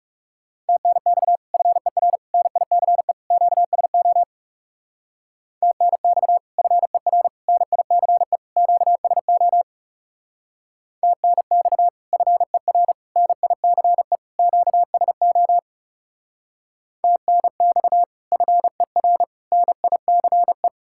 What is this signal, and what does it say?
Morse code, heard as a keyed tone, about 36 wpm, 700 Hz. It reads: TNX FER NICE QSO TNX FER NICE QSO TNX FER NICE QSO TNX FER NICE
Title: Speed-Racing Common QSO Phrases - 30wpm